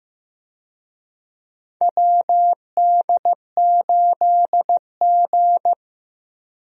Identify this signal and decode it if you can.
Morse code, heard as a keyed tone, about 16 wpm, 700 Hz. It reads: WD8G
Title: USA Call Signs - 4 Characters - Repeated 2x - 15wpm